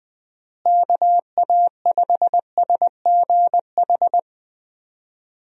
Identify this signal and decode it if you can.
Morse code, heard as a keyed tone, about 21 wpm, 700 Hz. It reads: KA5SGH
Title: USA Call Signs - 6 Characters - Repeated 2x - 20wpm